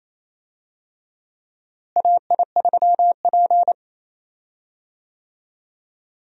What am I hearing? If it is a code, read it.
Morse code, heard as a keyed tone, about 28 wpm, 700 Hz. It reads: AI3P
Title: USA Call Signs - 4 Characters - Repeated 3x - 28wpm